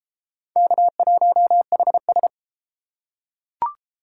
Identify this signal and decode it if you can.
Morse code, heard as a keyed tone, about 33 wpm, 700 Hz. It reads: K1HS